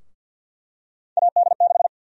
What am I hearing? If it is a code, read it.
Morse code, heard as a keyed tone, about 51 wpm, 700 Hz. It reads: AD6
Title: USA Call Sign Prefixes - Repeated 2x - 50wpm